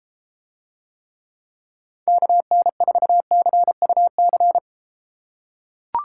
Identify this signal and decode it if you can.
Morse code, heard as a keyed tone, about 33 wpm, 700 Hz. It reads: KN4CUC